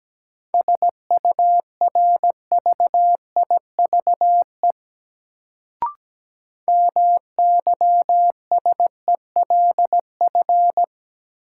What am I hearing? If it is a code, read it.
Morse code, heard as a keyed tone, about 17 wpm, 700 Hz. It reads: SURVIVE MYSELF
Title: Sentences from Top 1300 Words - New Word Drill - 17wpm